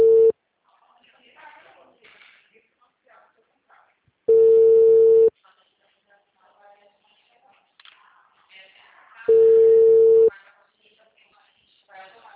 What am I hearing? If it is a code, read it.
Ringback tone (China, 450 Hz), with speech.